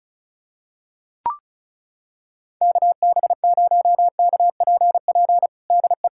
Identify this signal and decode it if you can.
Morse code, heard as a keyed tone, about 34 wpm, 700 Hz. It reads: KB0KPP DE
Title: USA Callsign DE Callsign 35wpm